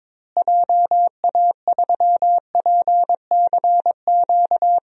Morse code, heard as a keyed tone, about 22 wpm, 700 Hz. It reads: JA3PCQ
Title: DXCC-100 Call Signs - 6 Characters - Repeated 3x - 22wpm